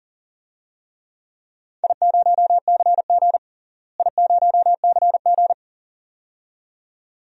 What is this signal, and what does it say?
Morse code, heard as a keyed tone, about 37 wpm, 700 Hz. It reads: I0CG I0CG
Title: DXCC-100 Call Signs - 4 Characters - Repeated 2x - 40wpm